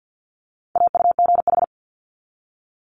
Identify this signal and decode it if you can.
Morse code, heard as a keyed tone, about 50 wpm, 690 Hz. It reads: AUGH
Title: Multi-letter Phonemes 50wpm